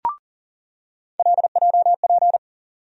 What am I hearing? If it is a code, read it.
Morse code, heard as a keyed tone, about 40 wpm, 700 Hz. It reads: LJP
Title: USA Call Sign Postfixes - 3 Letters 40wpm